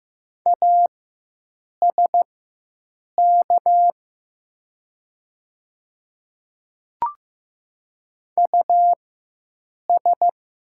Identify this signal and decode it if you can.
Morse code, heard as a keyed tone, about 15 wpm, spaced as for 7 wpm, 700 Hz. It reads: ASK US
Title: 4x Character Spacing - Top 200 Words - 15wpm